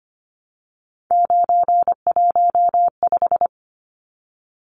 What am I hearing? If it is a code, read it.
Morse code, heard as a keyed tone, about 25 wpm, 700 Hz. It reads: 915
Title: Numbers - 3-Digit 25wpm